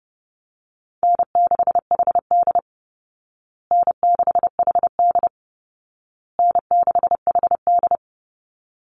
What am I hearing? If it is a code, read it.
Morse code, heard as a keyed tone, about 30 wpm, 700 Hz. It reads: N6HD N6HD N6HD
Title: Easy Call Signs Repeated 3x - 30wpm